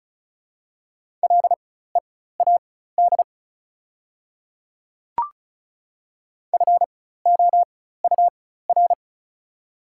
Morse code, heard as a keyed tone, about 36 wpm, spaced as for 16 wpm, 690 Hz. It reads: LEAD FOUR
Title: 4x Character Spacing - Top 500 Words - 35wpm